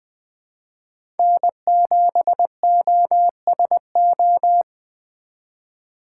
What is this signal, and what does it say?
Morse code, heard as a keyed tone, about 20 wpm, 700 Hz. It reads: N7OSO
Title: USA Call Signs - 5 Characters - 3x Character Spacing - 20wpm